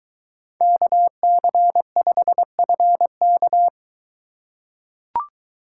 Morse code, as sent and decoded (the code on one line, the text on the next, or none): -.- -.-. ..... ..-. -.-
KC5FK